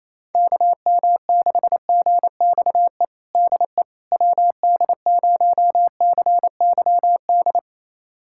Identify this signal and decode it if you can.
Morse code, heard as a keyed tone, about 28 wpm, 700 Hz. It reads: KM6GXE DE WD0CYD
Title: USA Callsign DE Callsign - 6 Characters - 28wpm